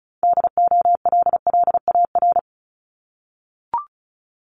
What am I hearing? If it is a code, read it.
Morse code, heard as a keyed tone, about 36 wpm, 700 Hz. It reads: DOLLAR
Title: Multi-letter Phonemes 35wpm